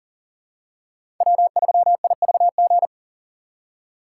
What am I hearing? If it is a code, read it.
Morse code, heard as a keyed tone, about 40 wpm, 700 Hz. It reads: W3IVG
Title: USA Call Signs - 5 Characters - 4x Character Spacing - 40wpm